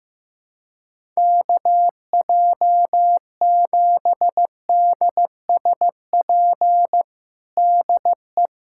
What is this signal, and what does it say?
Morse code, heard as a keyed tone, about 15 wpm, 700 Hz. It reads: KJ7DSP DE
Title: USA Callsign DE Callsign - 6 Characters - 15wpm